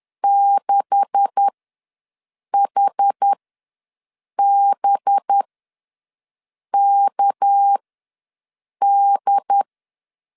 Morse code, heard as a keyed tone, about 10 wpm, spaced as for 6 wpm, 800 Hz. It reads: 6HBKD
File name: LEZIONI / 9 2867 ws 10wpm 4215